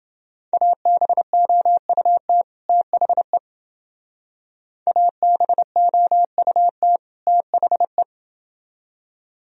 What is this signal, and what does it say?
Morse code, heard as a keyed tone, about 30 wpm, 700 Hz. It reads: ABOUT THE ABOUT THE